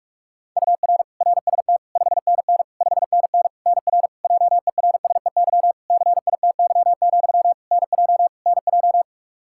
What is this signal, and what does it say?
Morse code, heard as a keyed tone, about 46 wpm, 700 Hz. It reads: UR RST 5NN 5NN NR JERSEY CITY, NJ NJ
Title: Practice QSO - Exchange 1 - Beginning - 45wpm